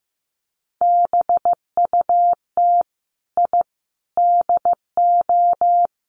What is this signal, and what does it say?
Morse code, heard as a keyed tone, about 15 wpm, 700 Hz. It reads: BUT I DO
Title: Sets of 3 Words 15wpm